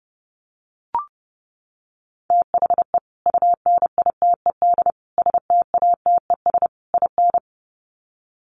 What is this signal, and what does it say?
Morse code, heard as a keyed tone, about 30 wpm, 700 Hz. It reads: THE UNITED STATES IN
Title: Sets of 4 Words 30wpm